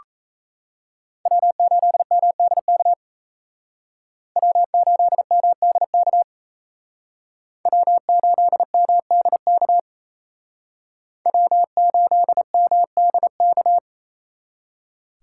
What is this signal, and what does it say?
Morse code, heard as a keyed tone, about 36 wpm, 700 Hz. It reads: W8MDK W8MDK W8MDK W8MDK